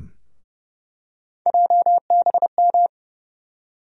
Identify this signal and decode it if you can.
Morse code, heard as a keyed tone, about 30 wpm, 700 Hz. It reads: JBM
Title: USA Call Sign Postfixes - 3 Letters 30wpm